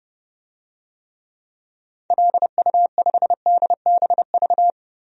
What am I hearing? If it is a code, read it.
Morse code, heard as a keyed tone, about 30 wpm, 700 Hz. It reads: LU5DBV
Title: DXCC-100 Call Signs - 6 Characters - Repeated 3x - 30wpm